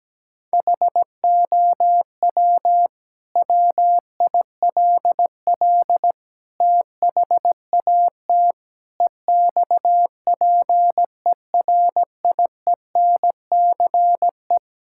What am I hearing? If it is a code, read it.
Morse code, heard as a keyed tone, about 17 wpm, 700 Hz. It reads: HOW WILL THAT EXPERIENCE
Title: Sentences from Top 1000 Words - Review Part 6 - 17wpm